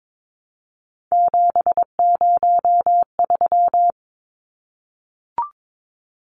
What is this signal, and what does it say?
Morse code, heard as a keyed tone, about 22 wpm, 700 Hz. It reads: 703
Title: Numbers - 3-Digit 22wpm